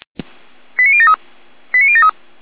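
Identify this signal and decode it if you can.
Five-tone selective call (ZVEI), heard as quick runs of five tones, tones of about 70 ms. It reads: 89073 89073
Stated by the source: ZVEI five-tone call received by SDR.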